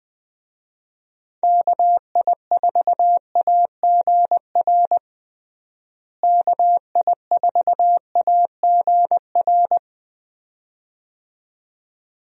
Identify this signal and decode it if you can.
Morse code, heard as a keyed tone, about 21 wpm, 700 Hz. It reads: KI4AGR KI4AGR
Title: USA Call Signs - 6 Characters - Repeated 2x - 20wpm